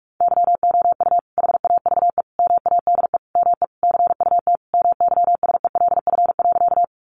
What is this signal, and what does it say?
Morse code, heard as a keyed tone, about 44 wpm, 700 Hz. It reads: YOU HAVE MADE ME CUT MYSELF.
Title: Sentences from Top 1300 Words 45wpm